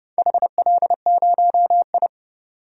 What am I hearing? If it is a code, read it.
Morse code, heard as a keyed tone, about 30 wpm, 700 Hz. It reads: HL0I
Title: DXCC-100 Call Signs - 4 Characters - 30wpm